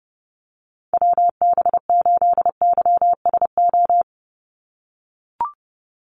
Morse code, heard as a keyed tone, about 30 wpm, 700 Hz. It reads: WB8YSO
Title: USA Call Signs 30wpm